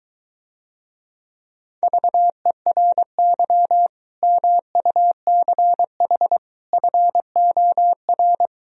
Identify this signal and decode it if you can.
Morse code, heard as a keyed tone, about 23 wpm, 700 Hz. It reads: VERY MUCH FOR